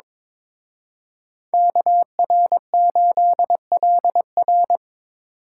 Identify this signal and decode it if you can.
Morse code, heard as a keyed tone, about 26 wpm, 700 Hz. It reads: KR8LR
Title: Speed Racing - USA 2x2 Call Signs 22wpm